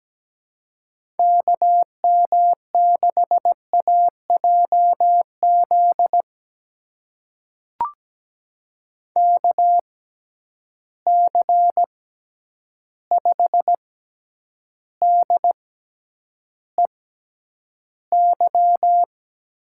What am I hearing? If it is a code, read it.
Morse code, heard as a keyed tone, about 17 wpm, spaced as for 9 wpm, 700 Hz. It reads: KM6AJZ KC5DEY
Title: USA Call Signs - 6 Characters - 6x Character Spacing - 17wpm